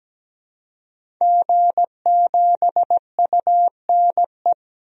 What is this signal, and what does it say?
Morse code, heard as a keyed tone, about 17 wpm, 700 Hz. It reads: G7UNE
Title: DXCC-100 Call Signs - 5 Characters - 17wpm